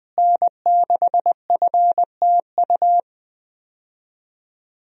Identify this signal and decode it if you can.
Morse code, heard as a keyed tone, about 20 wpm, 700 Hz. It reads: N6FTU
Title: USA Call Signs - 5 Characters - Repeated 3x - 20wpm